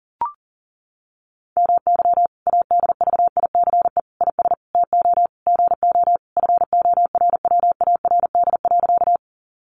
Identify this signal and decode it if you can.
Morse code, heard as a keyed tone, about 37 wpm, 700 Hz. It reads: MY ADVICE IS TO GO FORWARD.
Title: Sentences from Top 1300 Words 40wpm